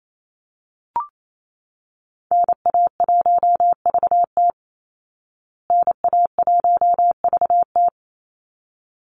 Morse code, heard as a keyed tone, about 28 wpm, 690 Hz. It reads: NA1VT NA1VT
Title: Call Signs Repeated 3x - 28wpm